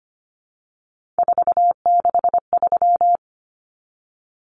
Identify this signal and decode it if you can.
Morse code, heard as a keyed tone, about 25 wpm, 700 Hz. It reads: 463
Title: Common Cut Numbers - 3-Digit 25wpm